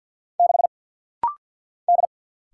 Morse code, heard as a keyed tone, about 50 wpm, 700 Hz. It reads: 6 D
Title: Single Letter-Number - Rapid-Fire 50wpm